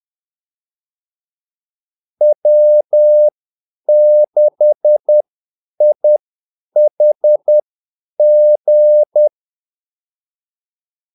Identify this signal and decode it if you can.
Morse code, heard as a keyed tone, about 10 wpm, spaced as for 8 wpm, 600 Hz. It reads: W6IHG
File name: Callsigns-10-8wpm